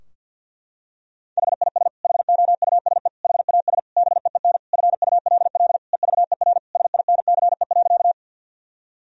Morse code, heard as a keyed tone, about 51 wpm, 700 Hz. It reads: HIS HOUSE HAS BEEN FULL EVER SINCE.
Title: Sentences from Top 1000 Words - Review Part 3 - 50wpm